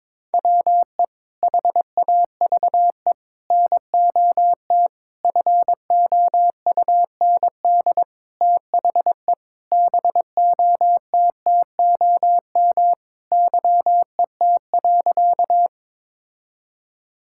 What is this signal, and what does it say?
Morse code, heard as a keyed tone, about 22 wpm, 700 Hz. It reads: WE HAVE NOT FOUND THE BOTTOM YET.
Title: Sentences from Top 1000 Words - Review Part 6 - 22wpm